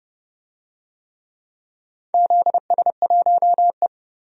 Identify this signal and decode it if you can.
Morse code, heard as a keyed tone, about 30 wpm, 700 Hz. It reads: ZS1E
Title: DXCC-100 Call Signs - 4 Characters - 30wpm